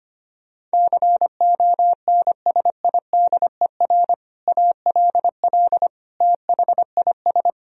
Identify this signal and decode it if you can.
Morse code, heard as a keyed tone, about 25 wpm, 700 Hz. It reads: CONSIDER ALL THIS
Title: Sentences from Top 1000 Words - Review Part 4 - 25wpm